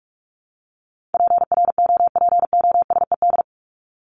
Morse code, heard as a keyed tone, about 44 wpm, 700 Hz. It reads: PROPOSED
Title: ED Suffix - Top 100 Words with Base Words 45wpm